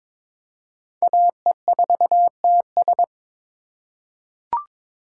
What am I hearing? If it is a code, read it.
Morse code, heard as a keyed tone, about 22 wpm, 700 Hz. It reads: AE4TS
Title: USA Call Signs - 5 Characters - 3x Character Spacing - 22wpm